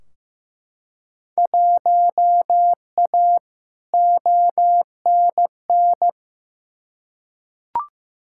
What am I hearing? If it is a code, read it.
Morse code, heard as a keyed tone, about 15 wpm, 700 Hz. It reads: 1A ONN
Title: Field Day Exchanges Encore - 15wpm